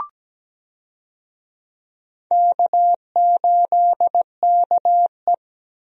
Morse code, heard as a keyed tone, about 17 wpm, 700 Hz. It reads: K8KE
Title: USA Call Signs - 4 Characters - Repeated 3x - 17wpm